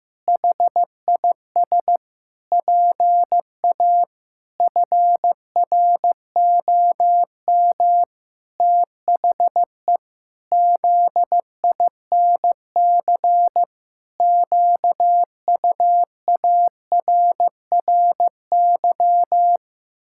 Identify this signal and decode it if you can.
Morse code, heard as a keyed tone, about 15 wpm, 700 Hz. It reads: HIS PA FROM THE ZINC QUARRY
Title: Pangram Challenge v2 - 15wpm